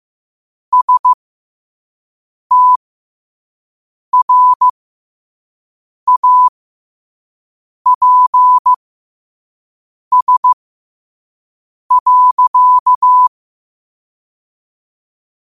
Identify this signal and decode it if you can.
Morse code, heard as a keyed tone, about 15 wpm, spaced as for 5 wpm, 1000 Hz. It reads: STRAPS.
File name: qotdfiles / 5wpm